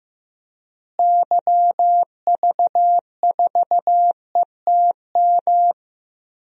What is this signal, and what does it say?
Morse code, heard as a keyed tone, about 15 wpm, 700 Hz. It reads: YV4ETM
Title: DXCC-100 Call Signs - 6 Characters - Repeated 3x - 15wpm